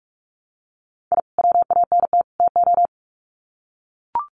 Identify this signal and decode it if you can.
Morse code, heard as a keyed tone, about 44 wpm, 700 Hz. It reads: I WANT TO
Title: Sets of 3 Words 45wpm